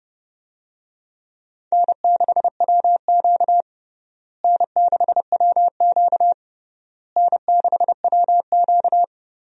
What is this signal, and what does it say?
Morse code, heard as a keyed tone, about 30 wpm, 700 Hz. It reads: N6WQ N6WQ N6WQ
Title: USA Call Signs - 4 Characters - Repeated 3x - 30wpm